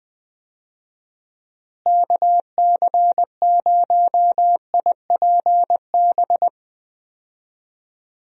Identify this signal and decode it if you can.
Morse code, heard as a keyed tone, about 21 wpm, 700 Hz. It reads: KC0IPB
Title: USA Call Signs - 6 Characters - Repeated 2x - 20wpm